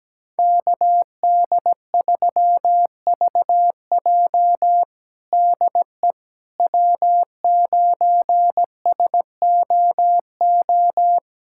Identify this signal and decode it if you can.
Morse code, heard as a keyed tone, about 17 wpm, 700 Hz. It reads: KD3VJ DE W9SOO
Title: USA Callsign DE Callsign - 5 Characters - 17wpm